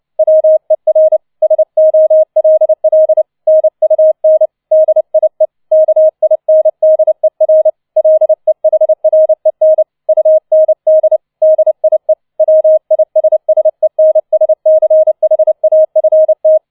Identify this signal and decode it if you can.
Morse code, heard as a keyed tone, about 29 wpm, 630 Hz. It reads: WER SOLL NUN DIE KINDER LEHREN UND DIE WISSENSCHAFT